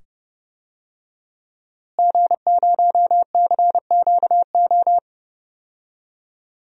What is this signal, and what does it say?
Morse code, heard as a keyed tone, about 30 wpm, 700 Hz. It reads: G0CQO
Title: DXCC-100 Call Signs - 5 Characters - Repeated 2x - 30wpm